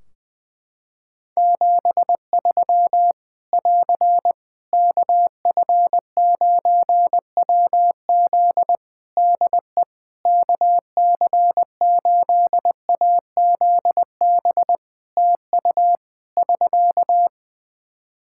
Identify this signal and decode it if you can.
Morse code, heard as a keyed tone, about 20 wpm, 700 Hz. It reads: 73 <AR> KF9WZ DE KC8AZB TU <SK>
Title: Practice QSO - Exchange 4 - Ending - 20wpm